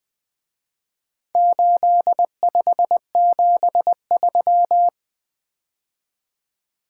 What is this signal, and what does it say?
Morse code, heard as a keyed tone, about 20 wpm, 700 Hz. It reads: 8573
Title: Speed-Racing - Four Letter-Number Combinations - 20wpm